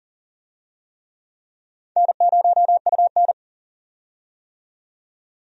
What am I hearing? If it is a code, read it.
Morse code, heard as a keyed tone, about 39 wpm, 700 Hz. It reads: N0UN
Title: USA Call Signs - 4 Characters - 40wpm